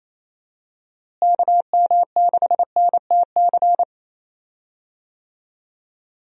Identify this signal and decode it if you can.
Morse code, heard as a keyed tone, about 28 wpm, 700 Hz. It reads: KM6NTC
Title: USA Call Signs - 6 Characters - Repeated 3x - 28wpm